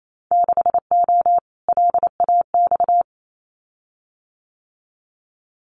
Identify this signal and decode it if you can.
Morse code, heard as a keyed tone, about 28 wpm, 700 Hz. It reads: 6O LAX
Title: Winter Field Day Exchanges Encore 28wpm